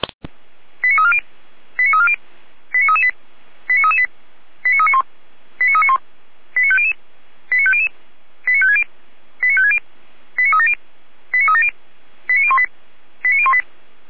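Five-tone selective call (ZVEI), heard as quick runs of five tones, tones of about 70 ms. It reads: 89340 89340 89338 89338 89391 89391 89500 89500 89560 89560 89370 89370 89018 89018
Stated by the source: ZVEI five-tone call 89336 received with an SDR.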